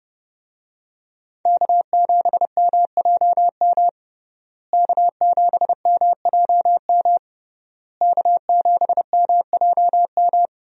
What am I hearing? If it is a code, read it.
Morse code, heard as a keyed tone, about 30 wpm, 700 Hz. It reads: K7MJM K7MJM K7MJM
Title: USA Call Signs - 5 Characters - Repeated 3x - 30wpm